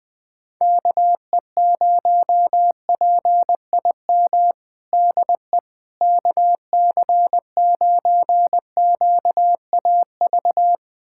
Morse code, heard as a keyed tone, about 21 wpm, 700 Hz. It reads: KE0PIM DE KC9QAV
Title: USA Callsign DE Callsign - 6 Characters - 20wpm